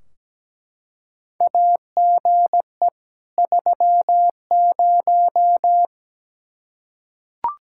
Morse code, heard as a keyed tone, about 17 wpm, 700 Hz. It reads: AGE 30
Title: Practice QSO - Exchange 3 - More Info - 17wpm